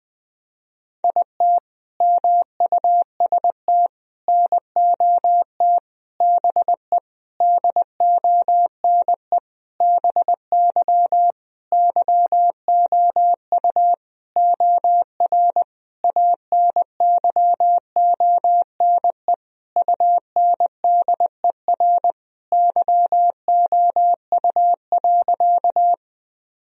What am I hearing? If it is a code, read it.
Morse code, heard as a keyed tone, about 20 wpm, 700 Hz. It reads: IT MUST NOT BE DONE BY YOU OR ANYONE UNDER YOU.
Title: Sentences from Top 1000 Words - Review Part 1 - 20wpm